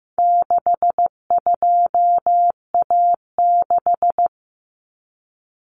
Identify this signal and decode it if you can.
Morse code, heard as a keyed tone, about 15 wpm, 700 Hz. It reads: 62A6
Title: Common Cut Numbers - 4-Digit 15wpm